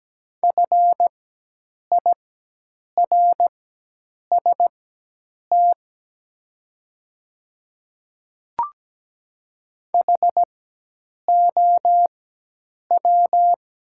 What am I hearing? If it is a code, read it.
Morse code, heard as a keyed tone, about 17 wpm, spaced as for 8 wpm, 690 Hz. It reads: FIRST HOW
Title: 4x Character Spacing - Top 100 Words - 17wpm